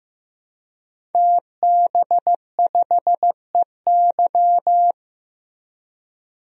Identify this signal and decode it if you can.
Morse code, heard as a keyed tone, about 15 wpm, 700 Hz. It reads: TB5EY
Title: DXCC-100 Call Signs - 5 Characters - Repeated 3x - 15wpm